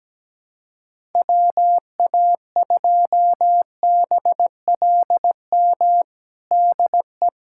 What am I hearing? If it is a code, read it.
Morse code, heard as a keyed tone, about 17 wpm, 700 Hz. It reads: WA2BLM DE
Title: USA Callsign DE Callsign 17wpm